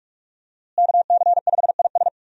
Morse code, heard as a keyed tone, about 46 wpm, 700 Hz. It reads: KC5IS
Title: USA Call Signs - 5 Characters - Repeated 3x - 45wpm